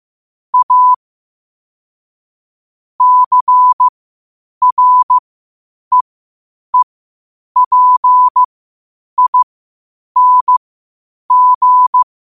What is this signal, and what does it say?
Morse code, heard as a keyed tone, about 15 wpm, spaced as for 8 wpm, 1000 Hz. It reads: A CREEPING
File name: qotdfiles / 8wpm